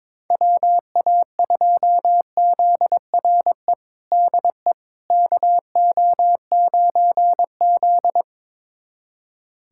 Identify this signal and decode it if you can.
Morse code, heard as a keyed tone, about 21 wpm, 700 Hz. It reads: WA2ZRE DE KO9Z
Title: USA Callsign DE Callsign 22wpm